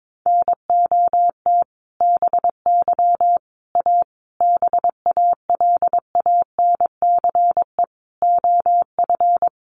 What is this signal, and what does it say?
Morse code, heard as a keyed tone, about 22 wpm, 700 Hz. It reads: NOT BY A BALANCE OF
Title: Sentences from Top 900 Words 22wpm